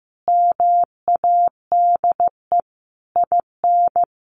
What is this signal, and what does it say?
Morse code, heard as a keyed tone, about 15 wpm, 700 Hz. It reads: MADE IN
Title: Sentences from Top 600 Words 15wpm